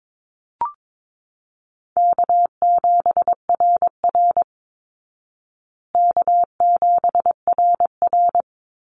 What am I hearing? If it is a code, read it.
Morse code, heard as a keyed tone, about 22 wpm, 700 Hz. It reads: K7RR K7RR
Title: Call Signs Repeated 3x - 22wpm